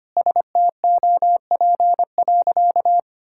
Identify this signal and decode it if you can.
Morse code, heard as a keyed tone, about 25 wpm, 700 Hz. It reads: STOP.
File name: practice / 25wpm / Sentences with 2-Word Phrases - 25wpm